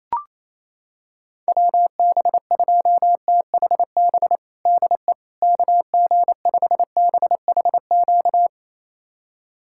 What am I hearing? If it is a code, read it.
Morse code, heard as a keyed tone, about 28 wpm, 700 Hz. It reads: WB2THB DE KG5BHQ
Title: USA Callsign DE Callsign 28wpm